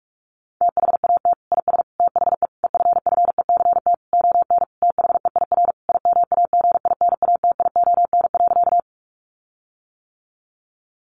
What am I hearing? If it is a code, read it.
Morse code, heard as a keyed tone, about 44 wpm, 700 Hz. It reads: THAT IS THE EFFECT ON THEIR IMAGINATION.
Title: Sentences from Top 1300 Words 45wpm